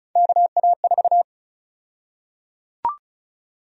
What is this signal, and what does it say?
Morse code, heard as a keyed tone, about 36 wpm, 700 Hz. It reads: KA4
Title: USA Call Sign Prefixes - 35wpm